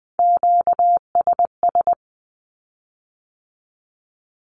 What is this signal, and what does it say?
Morse code, heard as a keyed tone, about 20 wpm, 700 Hz. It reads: QSS
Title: Q-Codes with Spoken Meaning - 20wpm